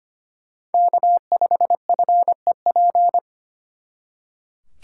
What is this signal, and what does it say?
Morse code, heard as a keyed tone, about 32 wpm, 700 Hz. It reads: K5FEP